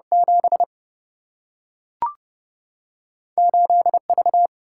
Morse code, heard as a keyed tone, about 30 wpm, 700 Hz. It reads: 7 8 V
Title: Single-Letter Single-Number 30wpm